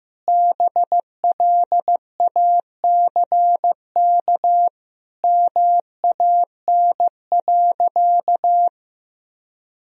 Morse code, heard as a keyed tone, about 15 wpm, 700 Hz. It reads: BLACK MAN.
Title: Sentences from Top 1000 Words - Review Part 2 - 15wpm